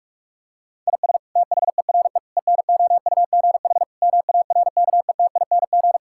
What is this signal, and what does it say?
Morse code, heard as a keyed tone, about 42 wpm, 700 Hz. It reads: IS THERE ENOUGH MARKETING
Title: Sentences from Top 1000 Words - Review Part 5 - 45wpm